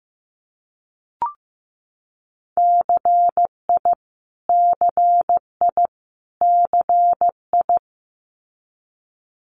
Morse code, heard as a keyed tone, about 15 wpm, 700 Hz. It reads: CI CI CI
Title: Multi-letter Phonemes Repeated 3x 15wpm